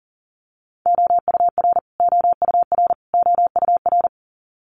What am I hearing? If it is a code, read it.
Morse code, heard as a keyed tone, about 40 wpm, 700 Hz. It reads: OUR OUR OUR
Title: Top 300 Words Repeated 40wpm